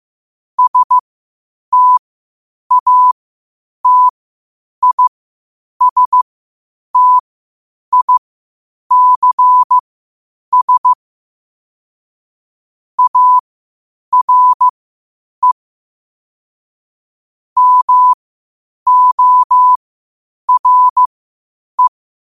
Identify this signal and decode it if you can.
Morse code, heard as a keyed tone, about 15 wpm, spaced as for 8 wpm, 1000 Hz. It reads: STATISTICS ARE MORE
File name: qotdfiles / 8wpm